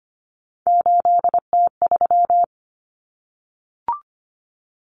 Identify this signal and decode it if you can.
Morse code, heard as a keyed tone, about 25 wpm, 700 Hz. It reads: 8T3
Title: Common Cut Numbers - 3-Digit 25wpm